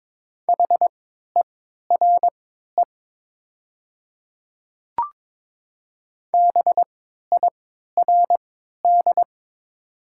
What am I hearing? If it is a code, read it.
Morse code, heard as a keyed tone, about 22 wpm, spaced as for 13 wpm, 700 Hz. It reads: HERE BIRD
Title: 3x Character Spacing - Top 500 Words - 22wpm